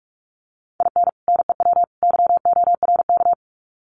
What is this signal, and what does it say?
Morse code, heard as a keyed tone, about 44 wpm, 700 Hz. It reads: IN NEW YORK
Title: Sets of 3 Words 45wpm